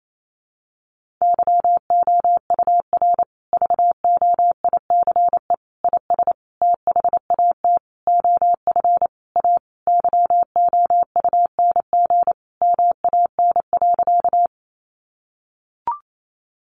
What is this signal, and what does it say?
Morse code, heard as a keyed tone, about 28 wpm, 700 Hz. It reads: YOUR VOICE IS THAT OF A YOUNG MAN.
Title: Sentences from Top 500 Words 28wpm